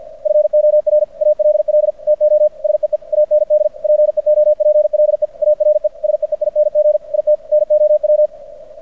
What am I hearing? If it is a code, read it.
Morse code, heard as a keyed tone, about 50 wpm, 590 Hz. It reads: FOR ALL TO SEE AND PEOPLE ARE SEEING IT NOW
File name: archived / CRSnet